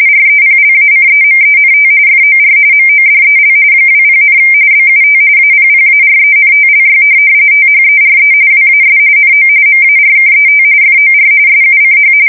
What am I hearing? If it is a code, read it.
RTTY, about 45.45 baud. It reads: REPLACEMENT FOR MODEL 20, BUT APPARENTLY THERE WAS NO MARKET FOR SUCH
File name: fsk170 / teletype-corp